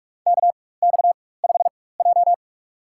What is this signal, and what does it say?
Morse code, heard as a keyed tone, about 46 wpm, spaced as for 21 wpm, 700 Hz. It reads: KX5J
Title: USA Call Signs - 4x Character Spacing - 45wpm